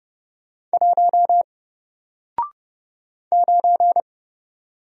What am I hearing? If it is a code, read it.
Morse code, heard as a keyed tone, about 30 wpm, 700 Hz. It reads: 1 9
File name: Single-Letter-Number-30wpm